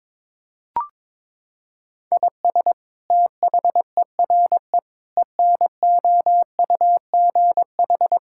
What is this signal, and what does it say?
Morse code, heard as a keyed tone, about 21 wpm, 700 Hz. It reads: IS THERE ENOUGH
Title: Sentences from Top 1000 Words - Review Part 5 - 22wpm